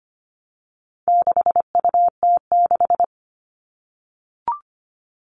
Morse code, heard as a keyed tone, about 25 wpm, 700 Hz. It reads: 6UT6
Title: All Cut Numbers - 4-Digit 25wpm